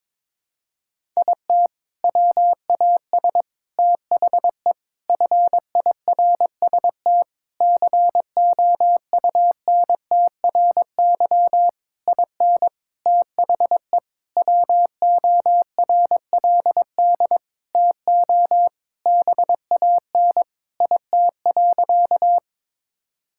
Morse code, heard as a keyed tone, about 22 wpm, 700 Hz. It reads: IT WAS THE FIRST COUNTRY IN THE WORLD TO BAN IT.
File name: practice / 22wpm / Sentences with 7-Word Phrases - 22wpm